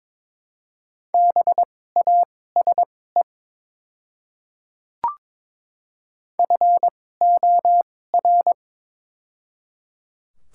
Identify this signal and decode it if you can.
Morse code, heard as a keyed tone, about 22 wpm, spaced as for 16 wpm, 700 Hz. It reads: BASE FOR
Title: 2x Character Spacing - Top 300 Words - 22wpm